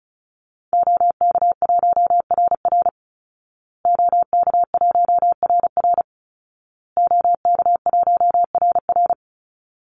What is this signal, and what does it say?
Morse code, heard as a keyed tone, about 34 wpm, 700 Hz. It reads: OK1RR OK1RR OK1RR
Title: Call Signs Repeated 2x - 35wpm